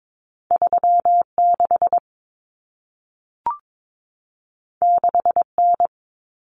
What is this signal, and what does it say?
Morse code, heard as a keyed tone, about 22 wpm, 700 Hz. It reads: 36 6N
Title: Common Cut Numbers - 2-Digit 22wpm